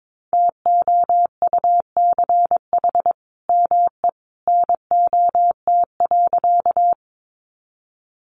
Morse code, heard as a keyed tone, about 22 wpm, 700 Hz. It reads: TOUCH ME NOT.
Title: Sentences from Top 1200 Words 22wpm